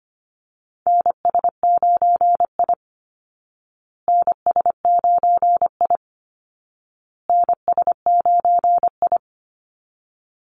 Morse code, heard as a keyed tone, about 25 wpm, 700 Hz. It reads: NS9I NS9I NS9I
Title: Call Signs Repeated 2x - 25wpm